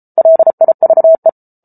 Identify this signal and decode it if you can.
Morse code, heard as a keyed tone, about 34 wpm, 650 Hz. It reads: LIVE